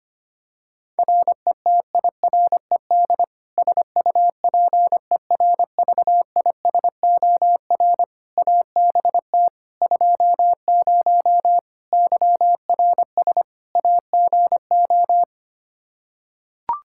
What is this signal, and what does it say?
Morse code, heard as a keyed tone, about 25 wpm, 700 Hz. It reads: RETIRED SUPERVISOR ABT 20 YRS AGO
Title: Practice QSO - Exchange 3 - More Info - 25wpm